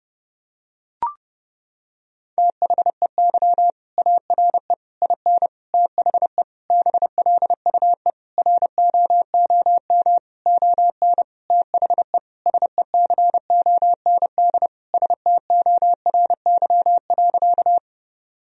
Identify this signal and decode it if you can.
Morse code, heard as a keyed tone, about 32 wpm, 700 Hz. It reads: THEY ARE IN THE BLUE ROOM ON THE SECOND STORY.
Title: Sentences from Top 1000 Words - Review Part 3 - 30wpm